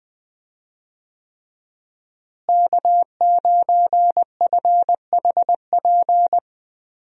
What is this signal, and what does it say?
Morse code, heard as a keyed tone, about 20 wpm, 700 Hz. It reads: K9FHP
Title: USA Call Signs - 5 Characters - Repeated 3x - 20wpm